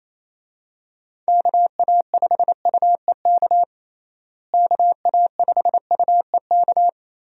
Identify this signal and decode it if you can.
Morse code, heard as a keyed tone, about 28 wpm, 700 Hz. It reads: KA5UEK KA5UEK
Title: USA Call Signs - 6 Characters - Repeated 3x - 28wpm